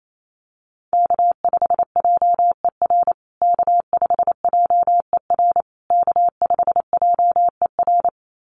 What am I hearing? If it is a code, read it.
Morse code, heard as a keyed tone, about 28 wpm, 700 Hz. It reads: K5JER K5JER K5JER
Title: USA Call Signs Repeated 3x 28wpm